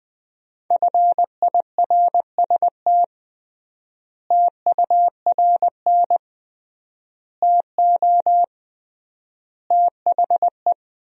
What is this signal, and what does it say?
Morse code, heard as a keyed tone, about 20 wpm, 700 Hz. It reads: FIRST TURN TO THE
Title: 3x Word-Spacing - Sentences from Top 300 Words - 20wpm